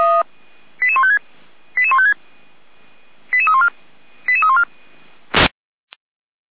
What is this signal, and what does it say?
Five-tone selective call (ZVEI), heard as quick runs of five tones, tones of about 70 ms. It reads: 88156 88156 88314 88314